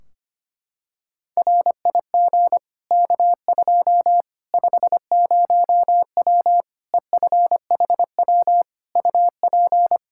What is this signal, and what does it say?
Morse code, heard as a keyed tone, about 25 wpm, 700 Hz. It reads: RIG K2 50W EFHW UP
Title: Practice QSO - Exchange 2 - Info - 25wpm